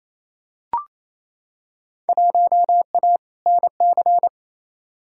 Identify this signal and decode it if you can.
Morse code, heard as a keyed tone, about 28 wpm, 700 Hz. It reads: 1A NC
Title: Field Day Exchanges Encore - 28wpm